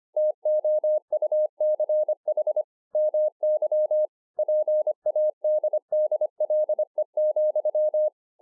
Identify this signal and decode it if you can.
Morse code, heard as a keyed tone, about 24 wpm, 610 Hz. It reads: TOUCH MY PADDLE,
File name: 2011-04-mp3 / THIS IS NOT MY SHACK 2 25WPM